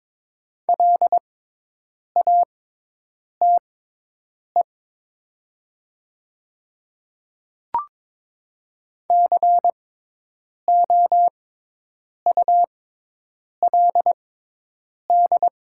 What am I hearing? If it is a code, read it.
Morse code, heard as a keyed tone, about 22 wpm, spaced as for 8 wpm, 700 Hz. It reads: LATE COULD
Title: 6x Character Spacing - Top 300 Words - 22wpm